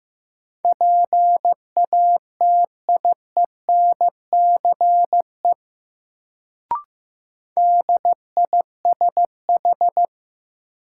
Morse code, heard as a keyed tone, about 15 wpm, 700 Hz. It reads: PATIENCE DISH
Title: Sentences from Top 1300 Words - New Word Drill - 15wpm